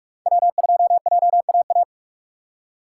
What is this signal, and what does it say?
Morse code, heard as a keyed tone, about 46 wpm, 700 Hz. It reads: W2JAA
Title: USA Call Signs - 5 Characters - 3x Character Spacing - 45wpm